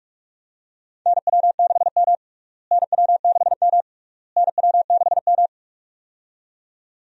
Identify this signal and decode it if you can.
Morse code, heard as a keyed tone, about 43 wpm, 700 Hz. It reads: NW6M NW6M NW6M
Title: USA Call Signs - 4 Characters - Repeated 3x - 45wpm